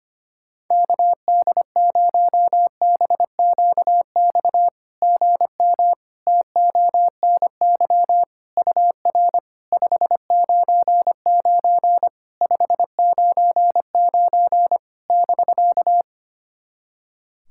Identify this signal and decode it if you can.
Morse code, heard as a keyed tone, about 25 wpm, 690 Hz. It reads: KD0BQX GM TONY UR 599 599 <BK>
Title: SOTA QSO Practice 25wpm